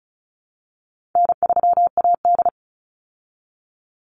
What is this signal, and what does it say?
Morse code, heard as a keyed tone, about 34 wpm, 700 Hz. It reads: N3AD
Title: Call Signs Repeated 3x - 35wpm